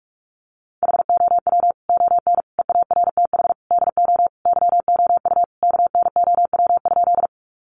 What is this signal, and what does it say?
Morse code, heard as a keyed tone, about 44 wpm, 700 Hz. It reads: HOW ON EARTH DO YOU KNOW?
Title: Sentences from Top 200 Words 45wpm